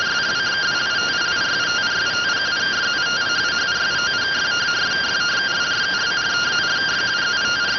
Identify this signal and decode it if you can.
Two-tone teleprinter signal, about 170 Hz shift.